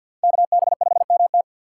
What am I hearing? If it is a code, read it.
Morse code, heard as a keyed tone, about 65 wpm, 700 Hz. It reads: KB5MT